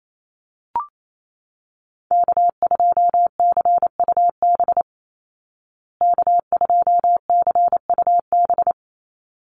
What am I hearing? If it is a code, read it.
Morse code, heard as a keyed tone, about 28 wpm, 700 Hz. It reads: K2CUB K2CUB
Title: Call Signs Repeated 3x - 28wpm